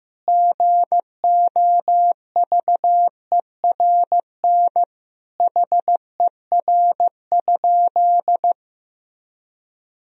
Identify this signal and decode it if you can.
Morse code, heard as a keyed tone, about 15 wpm, 700 Hz. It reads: GOVERN HER?
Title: Sentences from Top 1000 Words - Review Part 2 - 15wpm